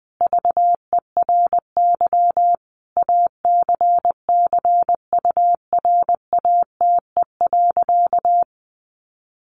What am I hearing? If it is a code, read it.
Morse code, heard as a keyed tone, about 20 wpm, 700 Hz. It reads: VERY ACCURATE.
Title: Sentences from Top 1200 Words 20wpm